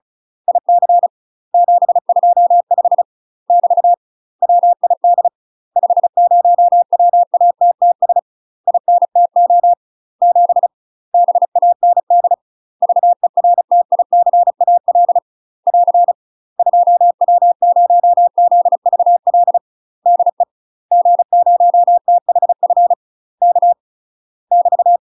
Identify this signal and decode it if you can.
Morse code, heard as a keyed tone, about 36 wpm, 700 Hz. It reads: IC 725 <BT> WID 50WATTS INTO 7 BAND VERTICAL <AR> 2W0ZVL DE G0THF K <BT>
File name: UK MASTER QSOs TRAINING/35WPM